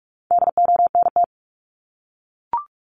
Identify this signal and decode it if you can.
Morse code, heard as a keyed tone, about 44 wpm, 700 Hz. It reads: DONT
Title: Top 300 Words 45wpm